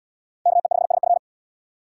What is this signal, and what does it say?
Morse code, heard as a keyed tone, about 70 wpm, 700 Hz. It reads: KE5SF